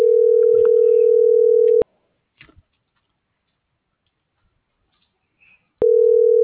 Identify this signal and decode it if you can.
Ringback tone (North America), 440 + 480 Hz.